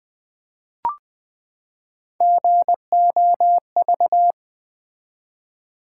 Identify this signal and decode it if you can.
Morse code, heard as a keyed tone, about 20 wpm, 700 Hz. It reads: GOV
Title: Build A Word - 3 Letters At A Time - 6-Letter Words - 20wpm